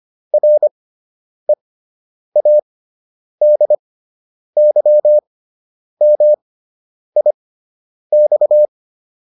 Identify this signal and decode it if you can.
Morse code, heard as a keyed tone, about 25 wpm, spaced as for 9 wpm, 600 Hz. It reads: READYMIX